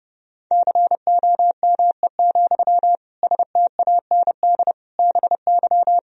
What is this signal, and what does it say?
Morse code, heard as a keyed tone, about 32 wpm, 700 Hz. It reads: COME, STAND BY
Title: Sentences from Top 1000 Words - Review Part 1 - 30wpm